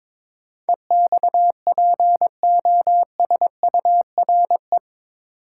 Morse code, heard as a keyed tone, about 22 wpm, 700 Hz. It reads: EXPOSURE
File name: practice / 22wpm / Eight Letter Words - 22wpm